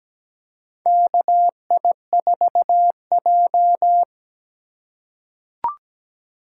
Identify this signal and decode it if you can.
Morse code, heard as a keyed tone, about 17 wpm, 700 Hz. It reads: KI4J
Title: USA Call Signs - 2x Character Spacing - 17wpm